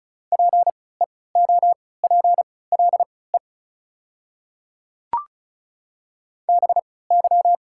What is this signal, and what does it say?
Morse code, heard as a keyed tone, about 36 wpm, spaced as for 20 wpm, 690 Hz. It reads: PEOPLE BY
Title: 3x Character Spacing - Top 100 Words - 35wpm